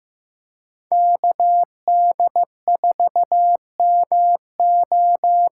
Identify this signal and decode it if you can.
Morse code, heard as a keyed tone, about 15 wpm, 700 Hz. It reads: KD4MO
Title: USA Call Signs - 6 Characters - 8x Character Spacing - 15wpm